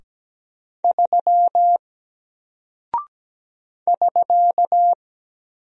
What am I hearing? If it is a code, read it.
Morse code, heard as a keyed tone, about 17 wpm, 700 Hz. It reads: 3 <SK>
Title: Single Character (Letter:Number:Punctuation:Prosign) Alt 17wpm